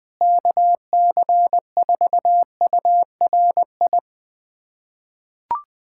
Morse code, heard as a keyed tone, about 20 wpm, 700 Hz. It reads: KC4URI
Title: USA Callsign DE Callsign 20wpm